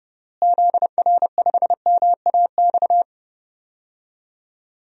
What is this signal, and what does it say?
Morse code, heard as a keyed tone, about 30 wpm, 700 Hz. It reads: ZR5MAX
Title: DXCC-100 Call Signs - 6 Characters - Repeated 3x - 30wpm